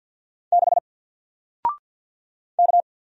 Morse code, heard as a keyed tone, about 50 wpm, 700 Hz. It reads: 6 K
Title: Single Letter-Number Alt 50wpm